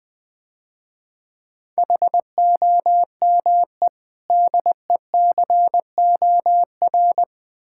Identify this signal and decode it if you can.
Morse code, heard as a keyed tone, about 22 wpm, 700 Hz. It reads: HOME DECOR
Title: Difficult Words - Speed-Racing with Hints - 20wpm